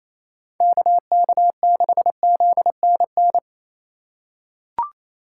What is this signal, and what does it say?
Morse code, heard as a keyed tone, about 28 wpm, 700 Hz. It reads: KK6ZNN
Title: USA Call Signs - 6 Characters - 2x Character Spacing - 28wpm